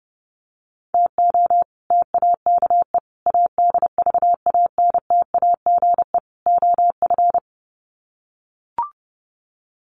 Morse code, heard as a keyed tone, about 30 wpm, 700 Hz. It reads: TO TAKE ADVANTAGE OF
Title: Sets of 4 Words 30wpm